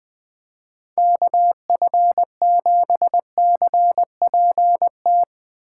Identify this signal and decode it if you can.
Morse code, heard as a keyed tone, about 20 wpm, 700 Hz. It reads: KF7CPT
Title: Speed Racing - USA Call Signs 17wpm